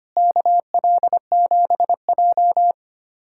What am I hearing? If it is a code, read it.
Morse code, heard as a keyed tone, about 25 wpm, 700 Hz. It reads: KL7J
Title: USA Callsign DE Callsign - 4 Characters - 25wpm